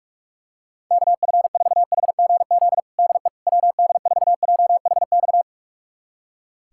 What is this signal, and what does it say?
Morse code, heard as a keyed tone, about 46 wpm, 700 Hz. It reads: KF4HGZ DE WD4JHX
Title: USA Callsign DE Callsign - 6 Characters - 45wpm